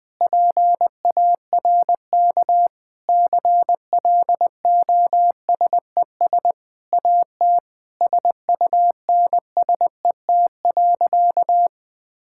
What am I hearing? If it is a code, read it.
Morse code, heard as a keyed tone, about 20 wpm, 700 Hz. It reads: PARK CLOSES AT SUNSET.
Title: Sentences with 7-Word Phrases - 20wpm